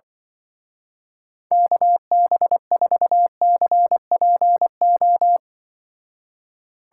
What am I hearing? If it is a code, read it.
Morse code, heard as a keyed tone, about 25 wpm, 700 Hz. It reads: KB4CPO
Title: Speed Racing - USA 2x3 Call Signs 20wpm